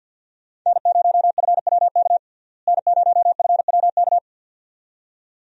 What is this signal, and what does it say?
Morse code, heard as a keyed tone, about 50 wpm, 700 Hz. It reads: N0FWK N0FWK
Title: USA Call Signs - 5 Characters - Repeated 2x - 50wpm